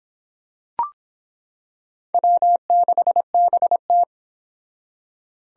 Morse code, heard as a keyed tone, about 26 wpm, 700 Hz. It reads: W6BT